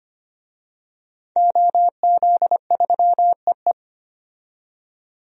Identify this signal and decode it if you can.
Morse code, heard as a keyed tone, about 25 wpm, 690 Hz. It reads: OZ3EE
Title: DXCC-100 Call Signs - 5 Characters - 25wpm